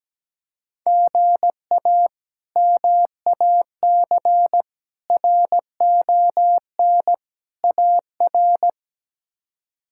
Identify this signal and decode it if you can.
Morse code, heard as a keyed tone, about 17 wpm, 700 Hz. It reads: GA MAC RON AR
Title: K1USN SST QSO Practice - 17wpm